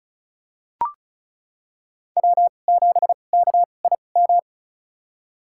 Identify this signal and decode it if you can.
Morse code, heard as a keyed tone, about 36 wpm, spaced as for 25 wpm, 700 Hz. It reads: W7KIM
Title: USA Call Signs - 2x Character Spacing - 35wpm